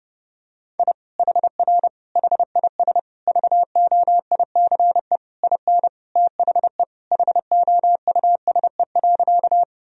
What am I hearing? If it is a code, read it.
Morse code, heard as a keyed tone, about 30 wpm, 700 Hz. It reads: I HR HIS VOICE IN THE HOUSE.
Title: Sentences from Top 500 Words and Abbreviations 30wpm